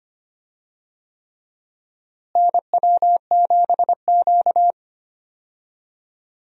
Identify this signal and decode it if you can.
Morse code, heard as a keyed tone, about 25 wpm, 700 Hz. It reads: NW7Q
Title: USA Call Signs - 4 Characters - Repeated 3x - 25wpm